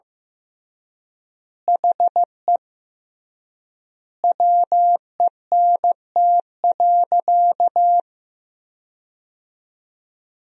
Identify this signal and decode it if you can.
Morse code, heard as a keyed tone, about 15 wpm, 700 Hz. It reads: HE WENT.
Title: 3x Word-Spacing - Sentences from Top 200 Words - 15wpm